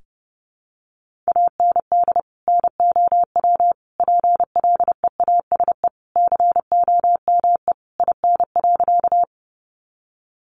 Morse code, heard as a keyed tone, about 30 wpm, 700 Hz. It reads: AND NOW PLEASE COME IN.
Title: Sentences from Top 1500 Words 30wpm